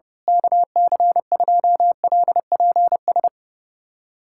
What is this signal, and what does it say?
Morse code, heard as a keyed tone, about 30 wpm, 700 Hz. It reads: KC2LPS
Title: USA Callsign DE Callsign 30wpm